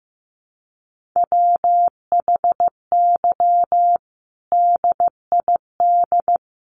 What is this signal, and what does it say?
Morse code, heard as a keyed tone, about 15 wpm, 700 Hz. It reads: WHY DID
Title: Sentences from Top 1500 Words 15wpm